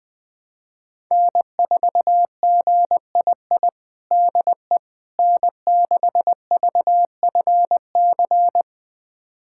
Morse code, heard as a keyed tone, about 21 wpm, 700 Hz. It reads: N4GII DE N6VFC
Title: USA Callsign DE Callsign - 5 Characters - 20wpm